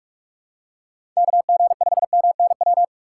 Morse code, heard as a keyed tone, about 44 wpm, 700 Hz. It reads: K G 5 M N W
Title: USA Call Signs - 6 Characters - 6x Character Spacing - 45wpm